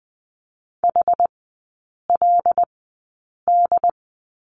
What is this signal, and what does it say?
Morse code, heard as a keyed tone, about 20 wpm, spaced as for 8 wpm, 700 Hz. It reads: HLD
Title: 074 Lesson 13e - Letter L - Three Characters 20wpm